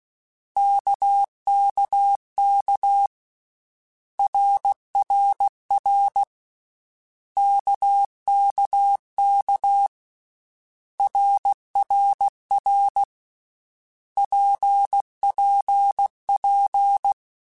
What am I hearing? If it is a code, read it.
Morse code, heard as a keyed tone, about 16 wpm, 800 Hz. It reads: KKK RRR KKK RRR PPP
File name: Basic Letter Groups By7Groups / RKPX / RKPX@13